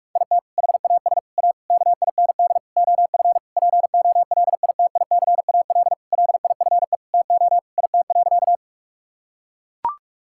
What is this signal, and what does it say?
Morse code, heard as a keyed tone, about 41 wpm, 700 Hz. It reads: IT HAS A KIND OF POLITICAL LIFE TO IT.
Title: Sentences from Top 1000 Words - Review Part 5 - 45wpm